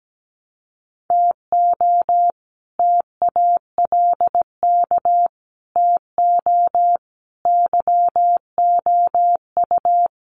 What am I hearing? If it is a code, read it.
Morse code, heard as a keyed tone, about 17 wpm, 700 Hz. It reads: TO TALK TO YOU
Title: Sets of 5 Words Encore 17wpm